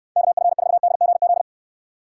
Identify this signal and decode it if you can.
Morse code, heard as a keyed tone, about 69 wpm, 700 Hz. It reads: KF4DGZ